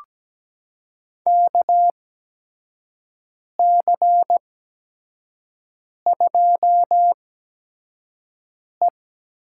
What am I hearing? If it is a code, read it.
Morse code, heard as a keyed tone, about 17 wpm, spaced as for 5 wpm, 700 Hz. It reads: KC2E
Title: USA Call Signs - 6 Characters - 8x Character Spacing - 17wpm